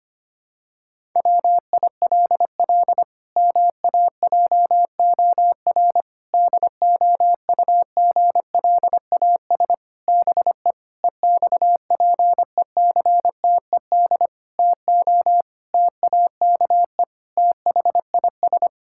Morse code, heard as a keyed tone, about 25 wpm, 700 Hz. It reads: WILL MAJOR DOUGLAS BE EXPECTED TO TAKE THIS
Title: Pangram Challenge v2 - 25wpm